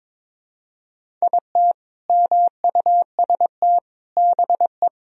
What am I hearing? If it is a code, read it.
Morse code, heard as a keyed tone, about 22 wpm, 700 Hz. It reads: IT MUST BE
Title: Sentences from Top 200 Words and Abbreviations 22wpm